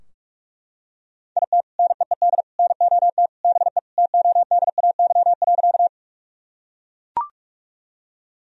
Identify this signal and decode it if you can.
Morse code, heard as a keyed tone, about 41 wpm, 700 Hz. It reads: IT NEED NOT BE TODAY.
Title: Sentences from Top 1000 Words - Review Part 4 - 45wpm